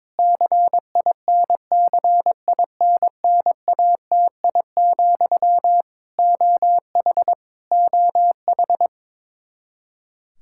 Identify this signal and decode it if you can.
Morse code, heard as a keyed tone, about 22 wpm, 700 Hz. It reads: CINCINNATI, OH OH
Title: Practice QSO - Exchange 1 - Beginning - 22wpm